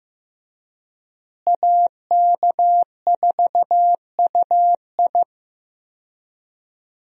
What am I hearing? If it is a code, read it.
Morse code, heard as a keyed tone, about 15 wpm, 700 Hz. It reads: AK4UI
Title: USA Call Signs - 5 Characters - Repeated 3x - 15wpm